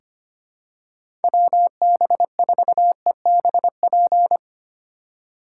Morse code, heard as a keyed tone, about 25 wpm, 700 Hz. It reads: WB4EBP
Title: USA Call Signs - 6 Characters - 8x Character Spacing - 25wpm